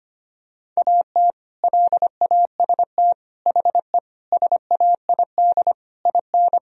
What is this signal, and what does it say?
Morse code, heard as a keyed tone, about 25 wpm, 700 Hz. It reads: AT LAST HE SAID IN
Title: Sentences from Top 1000 Words - Review Part 1 - 25wpm